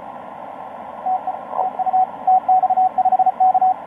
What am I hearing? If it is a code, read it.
Morse code, heard as a keyed tone, about 33 wpm, 750 Hz, with radio noise. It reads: TE TU TX5K
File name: tx5k dl4ny etc 12m cw 04mar2013 1521